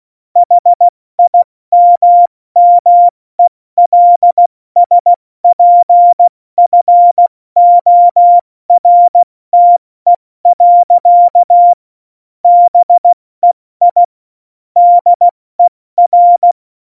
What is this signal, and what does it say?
Morse code, heard as a keyed tone, about 16 wpm, 700 Hz. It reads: HIMMELSPFORTE. BEI DER
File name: Morsekurs / Witze / Witz 26. Januar 2021, 70 F80